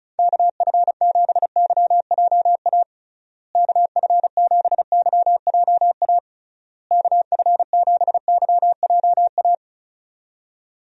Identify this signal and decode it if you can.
Morse code, heard as a keyed tone, about 34 wpm, 700 Hz. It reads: KF7YJA KF7YJA KF7YJA
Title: USA Call Signs - 6 Characters - Repeated 3x - 35wpm